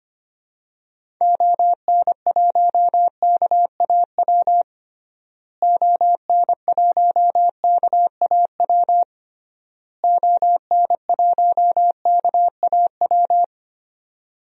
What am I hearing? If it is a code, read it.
Morse code, heard as a keyed tone, about 25 wpm, 700 Hz. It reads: ON1KAW ON1KAW ON1KAW
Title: DXCC-100 Call Signs - 6 Characters - Repeated 3x - 25wpm